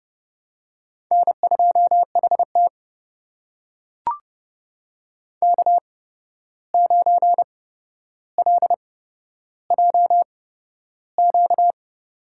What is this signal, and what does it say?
Morse code, heard as a keyed tone, about 30 wpm, spaced as for 12 wpm, 700 Hz. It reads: N2HT K9LJQ
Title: USA Call Signs - 8x Character Spacing - 30wpm